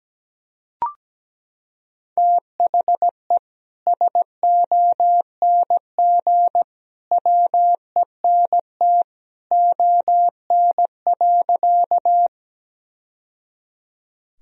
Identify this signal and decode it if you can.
Morse code, heard as a keyed tone, about 17 wpm, 700 Hz. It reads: THE SONG WENT ON.
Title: Sentences from Top 1000 Words - Review Part 2 - 17wpm